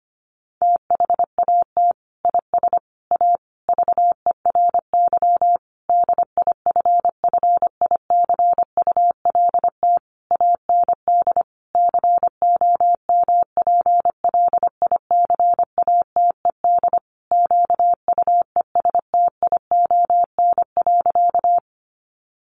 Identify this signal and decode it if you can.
Morse code, heard as a keyed tone, about 25 wpm, 700 Hz. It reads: THAT IS A VERY DIFFICULT AND COMPLICATED QUESTION.
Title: Sentences from Top 1200 Words 25wpm